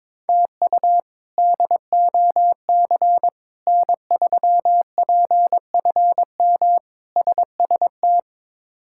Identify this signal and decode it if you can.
Morse code, heard as a keyed tone, about 22 wpm, 700 Hz. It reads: TU DOC N3PFM SST
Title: K1USN SST QSO Practice - 22wpm